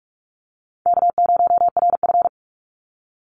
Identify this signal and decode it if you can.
Morse code, heard as a keyed tone, about 44 wpm, 700 Hz. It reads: K0RF
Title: Call Signs - 45wpm